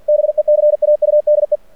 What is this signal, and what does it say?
Morse code, heard as a keyed tone, about 49 wpm, 600 Hz. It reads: BECAUDE
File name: archived / CRSnet